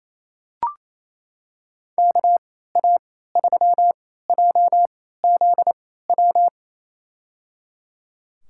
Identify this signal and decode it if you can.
Morse code, heard as a keyed tone, about 28 wpm, spaced as for 16 wpm, 700 Hz. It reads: KA3JZW
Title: USA Call Signs - 6 Characters - 3x Character Spacing - 28wpm